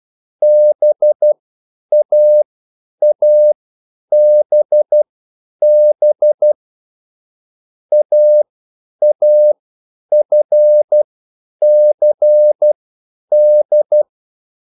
Morse code, heard as a keyed tone, about 12 wpm, spaced as for 9 wpm, 600 Hz. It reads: BAABB AAFCD
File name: QSL / CW lesson